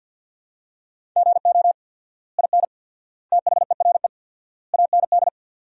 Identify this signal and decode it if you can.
Morse code, heard as a keyed tone, about 53 wpm, 700 Hz. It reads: GO IN THERE AND
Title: 4x Word-Spacing - Sentences from Top 500 Words - 50wpm